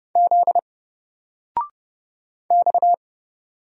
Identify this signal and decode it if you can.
Morse code, heard as a keyed tone, about 29 wpm, 700 Hz. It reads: Z X
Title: Single Character (Letter:Number:Punctuation:Prosign) 30wpm